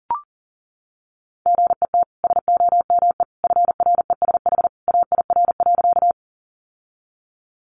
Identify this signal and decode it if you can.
Morse code, heard as a keyed tone, about 40 wpm, 700 Hz. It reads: GET SOME FRESH AIR.
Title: Sentences from Top 1200 Words 40wpm